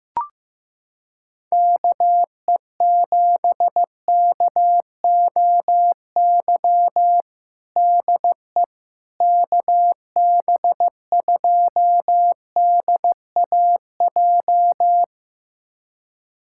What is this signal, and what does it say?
Morse code, heard as a keyed tone, about 15 wpm, 700 Hz. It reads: KE7KOY DE KB2DAJ
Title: USA Callsign DE Callsign - 6 Characters - 15wpm